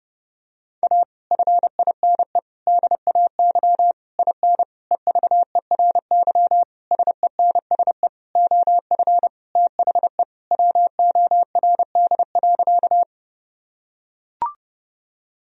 Morse code, heard as a keyed tone, about 30 wpm, 700 Hz. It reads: A FINE DAY IN EVERY SENSE OF THE WORD.
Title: Sentences from Top 1000 Words - Review Part 4 - 30wpm